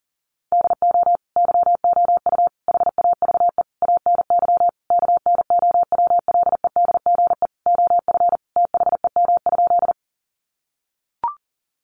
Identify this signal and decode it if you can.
Morse code, heard as a keyed tone, about 40 wpm, 700 Hz. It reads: DO YOU HAVE ANY KNOWLEDGE OF THEM?
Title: Sentences from Top 600 Words 40wpm